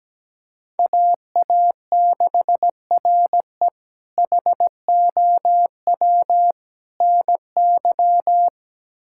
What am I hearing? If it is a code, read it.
Morse code, heard as a keyed tone, about 17 wpm, 700 Hz. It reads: AA6RE HOW NY
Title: K1USN SST QSO Practice - 17wpm